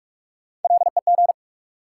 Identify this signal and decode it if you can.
Morse code, heard as a keyed tone, about 46 wpm, 700 Hz. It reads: LEG
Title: Three Letter Words - 45wpm